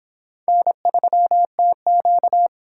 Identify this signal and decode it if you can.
Morse code, heard as a keyed tone, about 26 wpm, 700 Hz. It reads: N3TQ